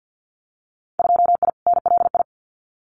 Morse code, heard as a keyed tone, about 50 wpm, 700 Hz. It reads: 2I NLI
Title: Winter Field Day Exchanges 50wpm